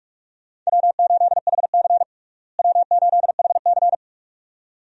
Morse code, heard as a keyed tone, about 46 wpm, 700 Hz. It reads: W8HC W8HC
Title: USA Call Signs - 4 Characters - Repeated 2x - 45wpm